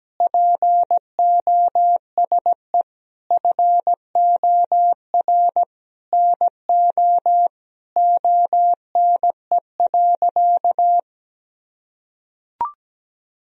Morse code, heard as a keyed tone, about 17 wpm, 700 Hz. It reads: POSE FOR NO ONE.
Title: Sentences from Top 1000 Words - Review Part 2 - 17wpm